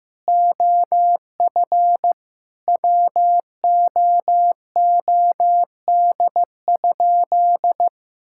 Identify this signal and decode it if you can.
Morse code, heard as a keyed tone, about 16 wpm, 700 Hz. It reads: OF WOOD?
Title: Sentences from Top 1600 Words 15wpm